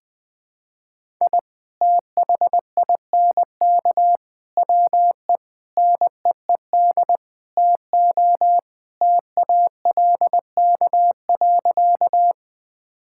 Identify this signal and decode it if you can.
Morse code, heard as a keyed tone, about 21 wpm, 700 Hz. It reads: I THINK WE NEED TO TALK.
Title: Sentences with 5-Word Phrases - 20wpm